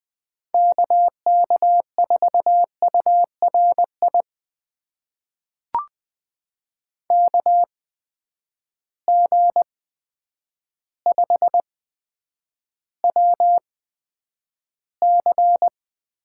Morse code, heard as a keyed tone, about 20 wpm, spaced as for 9 wpm, 700 Hz. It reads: KK4URI KG5WC
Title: USA Call Signs - 6 Characters - 8x Character Spacing - 20wpm